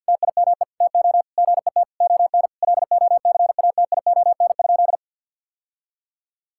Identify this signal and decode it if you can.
Morse code, heard as a keyed tone, about 47 wpm, 700 Hz. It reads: TIME TO GET ON LOCATION?
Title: Sentences from Top 1000 Words - Review Part 6 - 50wpm